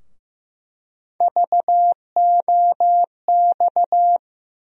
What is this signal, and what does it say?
Morse code, heard as a keyed tone, about 15 wpm, 700 Hz. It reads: VOX
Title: USA Call Sign Postfixes - 3 Letters 15wpm